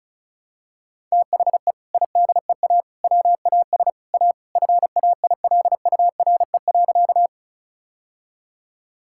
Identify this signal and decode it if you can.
Morse code, heard as a keyed tone, about 34 wpm, 700 Hz. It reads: THE IDEA WAS A FAILURE.
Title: Sentences from Top 1000 Words - Review Part 6 - 35wpm